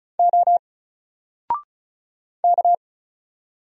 Morse code, heard as a keyed tone, about 33 wpm, 700 Hz. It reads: O K
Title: Single Letter-Number 35wpm